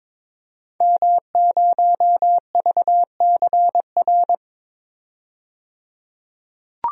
Morse code, heard as a keyed tone, about 22 wpm, 700 Hz. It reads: M0VCR
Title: DXCC-100 Call Signs - 5 Characters - 22wpm